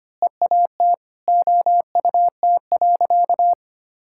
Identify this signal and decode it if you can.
Morse code, heard as a keyed tone, about 25 wpm, 700 Hz. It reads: EAT OUT.
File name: practice / 25wpm / Sentences with 4-Word Phrases - 25wpm